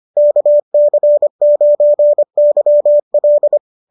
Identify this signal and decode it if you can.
Morse code, heard as a keyed tone, about 25 wpm, 590 Hz. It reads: KC9YL